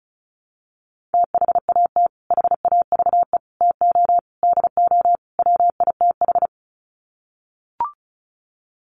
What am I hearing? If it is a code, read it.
Morse code, heard as a keyed tone, about 34 wpm, 700 Hz. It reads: THAT HAVE TO DO WITH
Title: Sets of 5 Words Encore2 35wpm